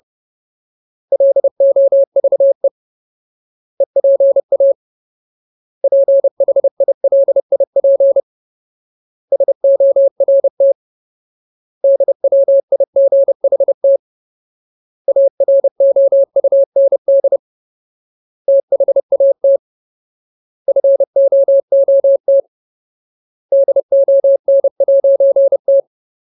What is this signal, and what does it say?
Morse code, heard as a keyed tone, about 30 wpm, 550 Hz. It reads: LOVE EPA PHILIP SORT DWIGHT AROUND THAT FOOT DON'T
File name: Ultimate Word List / 30WPM Ultimate